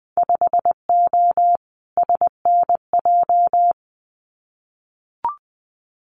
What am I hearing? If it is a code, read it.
Morse code, heard as a keyed tone, about 20 wpm, 700 Hz. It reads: 5O SNJ
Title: Winter Field Day Exchanges Encore 20wpm